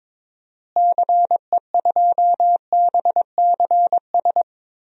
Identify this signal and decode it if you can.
Morse code, heard as a keyed tone, about 21 wpm, 700 Hz. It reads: CE2BCS
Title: DXCC-100 Call Signs - 6 Characters - Repeated 2x - 22wpm